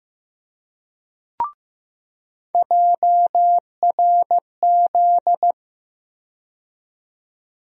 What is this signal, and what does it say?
Morse code, heard as a keyed tone, about 15 wpm, 700 Hz. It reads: JRZ
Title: USA Call Sign Postfixes - 3 Letters 15wpm